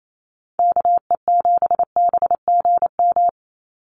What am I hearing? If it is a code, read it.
Morse code, heard as a keyed tone, about 28 wpm, 700 Hz. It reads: KE7BGM
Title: NA SOTA Activator Callsigns 28wpm